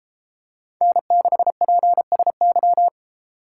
Morse code, heard as a keyed tone, about 32 wpm, 700 Hz. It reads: N6PSY